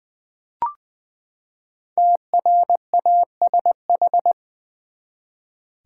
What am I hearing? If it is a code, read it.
Morse code, heard as a keyed tone, about 20 wpm, 700 Hz. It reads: TRASH
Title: Five Letter Words - 20wpm